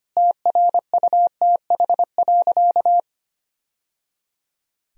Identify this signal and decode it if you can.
Morse code, heard as a keyed tone, about 25 wpm, 700 Hz. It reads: TRUTH.
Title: Sentences from Top 1000 Words - Review Part 4 - 25wpm